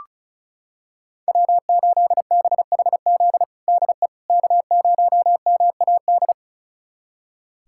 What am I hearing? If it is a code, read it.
Morse code, heard as a keyed tone, about 36 wpm, 690 Hz. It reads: W8BHZ DE K0MAD
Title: USA Callsign DE Callsign - 5 Characters - 35wpm